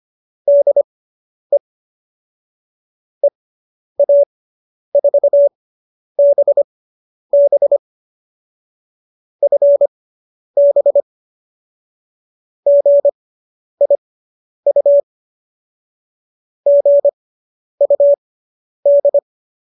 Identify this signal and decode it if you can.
Morse code, heard as a keyed tone, about 25 wpm, spaced as for 10 wpm, 570 Hz. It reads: DE EA4BB FB GIU GUD